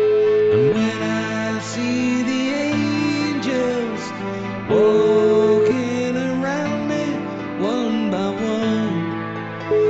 Ringback tone (Europe, 425 Hz), with noise.